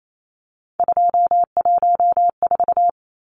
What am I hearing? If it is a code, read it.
Morse code, heard as a keyed tone, about 28 wpm, 700 Hz. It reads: 214
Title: Numbers - 3-Digit 28wpm